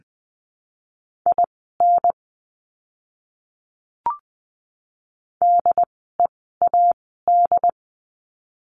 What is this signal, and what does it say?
Morse code, heard as a keyed tone, about 20 wpm, spaced as for 14 wpm, 700 Hz. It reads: IN DEAD
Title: 075 Lesson 13f - Letter L - Words 20wpm